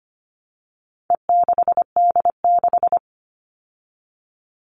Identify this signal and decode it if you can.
Morse code, heard as a keyed tone, about 25 wpm, 700 Hz. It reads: E6D6
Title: All Cut Numbers - 4-Digit 25wpm